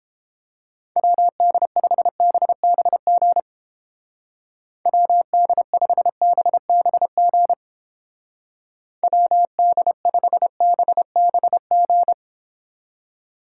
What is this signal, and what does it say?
Morse code, heard as a keyed tone, about 30 wpm, 700 Hz. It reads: WD5BBG WD5BBG WD5BBG